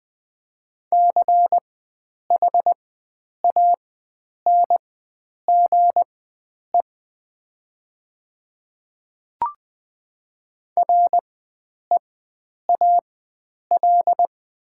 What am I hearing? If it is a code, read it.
Morse code, heard as a keyed tone, about 20 wpm, spaced as for 9 wpm, 700 Hz. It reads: CHANGE REAL
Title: 4x Character Spacing - Top 500 Words - 20wpm